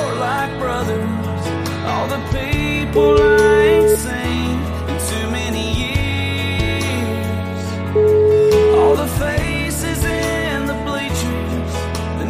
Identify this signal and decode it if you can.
Ringback tone (Europe, 425 Hz), with noise.